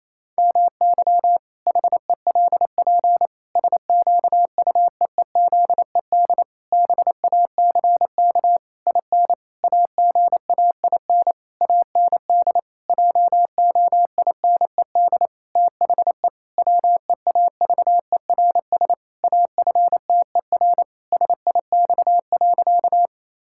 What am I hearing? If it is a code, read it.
Morse code, heard as a keyed tone, about 28 wpm, 700 Hz. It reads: MY HELP SQUEEZED BACK IN AGAIN AND JOINED THE WEAVERS AFTER SIX.
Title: Pangram Challenge v2 - 28wpm